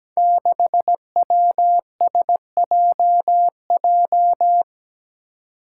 Morse code, heard as a keyed tone, about 17 wpm, 700 Hz. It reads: 6WSJJ
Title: Five Challenging Letter-Number Combinations 17wpm